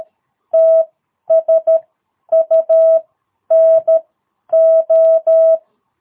Morse code, heard as a keyed tone, about 13 wpm, spaced as for 10 wpm, 660 Hz. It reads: TSUNO
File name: jh8cbh / 2022cw